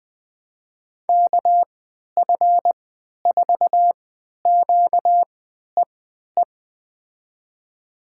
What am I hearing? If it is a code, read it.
Morse code, heard as a keyed tone, about 20 wpm, spaced as for 11 wpm, 700 Hz. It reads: KF4QEE
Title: USA Call Signs - 6 Characters - 3x Character Spacing - 20wpm